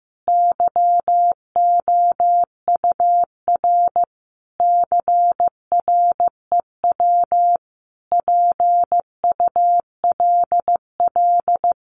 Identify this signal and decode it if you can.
Morse code, heard as a keyed tone, about 15 wpm, 700 Hz. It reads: YOUR CREW PULL
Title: Sentences from Top 1500 Words 15wpm